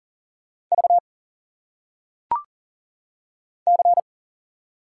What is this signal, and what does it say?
Morse code, heard as a keyed tone, about 39 wpm, 700 Hz. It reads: V C
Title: Single Letters - Phonetic Pronunciation 40wpm